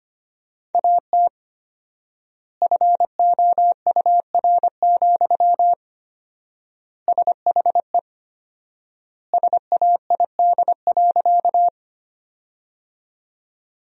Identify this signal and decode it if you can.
Morse code, heard as a keyed tone, about 25 wpm, 700 Hz. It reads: AT FOUR, SHE SAID.
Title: 4x Word-Spacing - Sentences from Top 300 Words - 25wpm